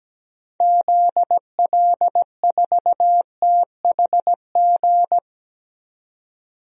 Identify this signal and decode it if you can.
Morse code, heard as a keyed tone, about 17 wpm, 700 Hz. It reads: ZL4THG
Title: DXCC-100 Call Signs - 6 Characters - 17wpm